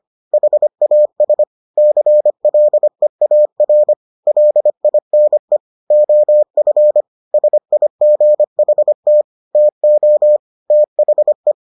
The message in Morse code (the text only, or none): HAS CLEAR LINE OF SIGHT TO THE